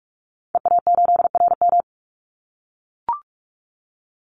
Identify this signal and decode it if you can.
Morse code, heard as a keyed tone, about 44 wpm, 700 Hz. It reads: EA8RM
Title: Call Signs - 45wpm